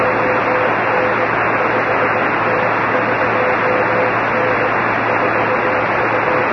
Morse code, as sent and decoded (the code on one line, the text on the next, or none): none